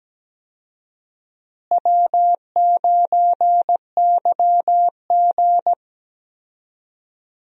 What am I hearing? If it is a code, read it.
Morse code, heard as a keyed tone, about 17 wpm, 700 Hz. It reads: W9YG
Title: USA Call Signs - 4 Characters - Repeated 2x - 17wpm